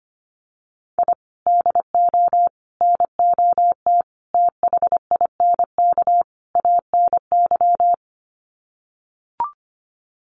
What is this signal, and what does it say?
Morse code, heard as a keyed tone, about 25 wpm, 700 Hz. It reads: I DO NOT THINK ANY
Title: Sets of 5 Words Encore 25wpm